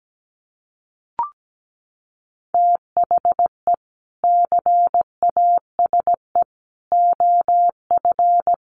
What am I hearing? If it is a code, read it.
Morse code, heard as a keyed tone, about 17 wpm, 700 Hz. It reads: THE CASE OF
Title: Sets of 4 Words 17wpm